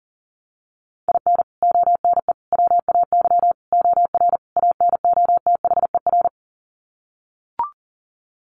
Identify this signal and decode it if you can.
Morse code, heard as a keyed tone, about 40 wpm, 700 Hz. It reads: IN ONE WAY OR ANOTHER
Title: Sets of 5 Words Encore 40wpm